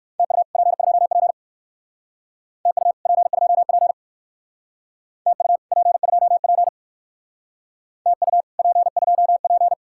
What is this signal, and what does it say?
Morse code, heard as a keyed tone, about 58 wpm, 700 Hz. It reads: TU P2P TU P2P TU P2P TU P2P